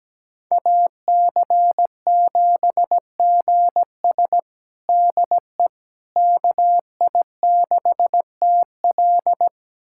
Morse code, heard as a keyed tone, about 17 wpm, 700 Hz. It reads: AC7GS DE KI6TL
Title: USA Callsign DE Callsign - 5 Characters - 17wpm